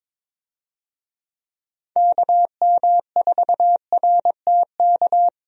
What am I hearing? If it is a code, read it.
Morse code, heard as a keyed tone, about 22 wpm, 700 Hz. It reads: KM4RTK
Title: USA Call Signs - 6 Characters - 22wpm